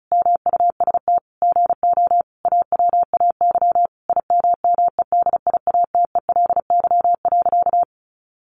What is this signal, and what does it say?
Morse code, heard as a keyed tone, about 34 wpm, 700 Hz. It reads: MUST GO AWAY IMMEDIATELY.
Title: Sentences from Top 1100 Words 35wpm